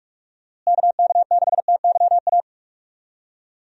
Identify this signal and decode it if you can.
Morse code, heard as a keyed tone, about 42 wpm, 700 Hz. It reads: KK6TYA
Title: USA Call Signs - 6 Characters - Repeated 2x - 45wpm